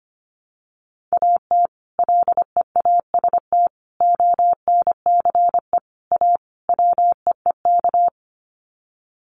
Morse code, heard as a keyed tone, about 25 wpm, 700 Hz. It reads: AT LEAST ONCE A WEEK
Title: Sets of 5 Words Encore2 25wpm